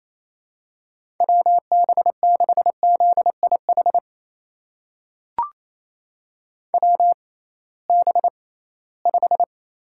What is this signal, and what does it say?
Morse code, heard as a keyed tone, about 27 wpm, spaced as for 18 wpm, 700 Hz. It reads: WB6ZIH WB5
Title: USA Call Signs - 6 Characters - 6x Character Spacing - 28wpm